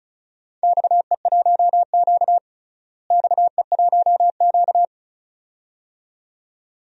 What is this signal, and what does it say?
Morse code, heard as a keyed tone, about 36 wpm, 700 Hz. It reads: XE1Q XE1Q
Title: DXCC-100 Call Signs - 4 Characters - Repeated 2x - 35wpm